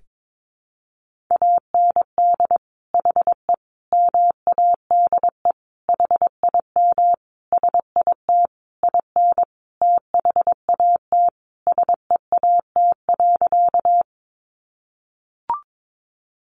Morse code, heard as a keyed tone, about 22 wpm, 700 Hz. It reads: AND HE MADE HIM SIT IN THAT SEAT.
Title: Sentences from Top 1500 Words 22wpm